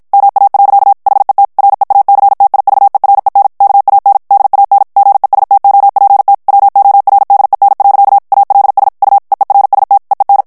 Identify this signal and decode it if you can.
Morse code, heard as a keyed tone, about 52 wpm, 800 Hz. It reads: MA. HET REACTIVEREN KAN DAN GESTOPT WORDEN. ALS U EERST EEN
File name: HOLLANDAIS texte clair 2 vitesse 50 mots minute